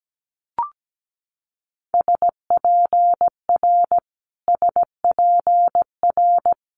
Morse code, heard as a keyed tone, about 17 wpm, 700 Hz. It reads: SPR SPR
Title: Multi-letter Phonemes Repeated 3x 17wpm